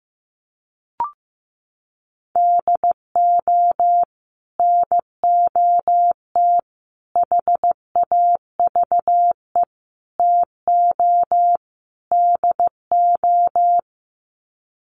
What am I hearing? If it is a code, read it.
Morse code, heard as a keyed tone, about 15 wpm, 700 Hz. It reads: DO NOT HAVE TO DO
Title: Sets of 5 Words 15wpm